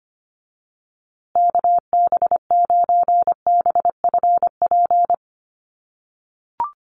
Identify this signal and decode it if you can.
Morse code, heard as a keyed tone, about 25 wpm, 700 Hz. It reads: KB9BFP
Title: USA Call Signs Repeated 3x 25wpm